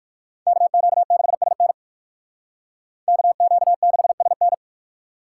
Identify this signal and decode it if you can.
Morse code, heard as a keyed tone, about 62 wpm, 700 Hz. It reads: KQ6SN KQ6SN